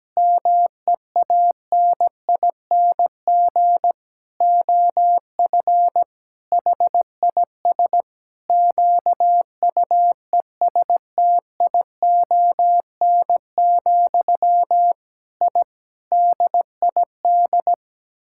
Morse code, heard as a keyed tone, about 17 wpm, 700 Hz. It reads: MEANING OF HIS QUESTION, I DID
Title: Sentences from Top 1000 Words - Review Part 6 - 17wpm